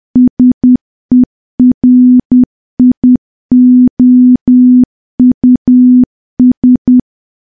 Morse code, heard as a keyed tone, about 10 wpm, 250 Hz. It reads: SERIOUS
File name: morse / morse-training-10WPM